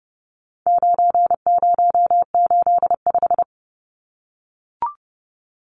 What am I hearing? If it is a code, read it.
Morse code, heard as a keyed tone, about 30 wpm, 700 Hz. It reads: 9085
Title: Numbers - 4-Digit 30wpm